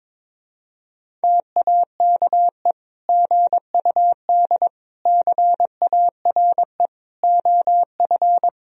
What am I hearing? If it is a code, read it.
Morse code, heard as a keyed tone, about 22 wpm, 700 Hz. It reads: TAKE GUD CARE OF
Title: Sentences from Top 400 Words and Abbreviations 22wpm